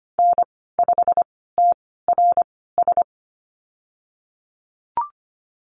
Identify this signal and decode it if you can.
Morse code, heard as a keyed tone, about 25 wpm, spaced as for 16 wpm, 700 Hz. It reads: N5TRS
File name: Lesson 15g - Number 5 - Call Signs 25wpm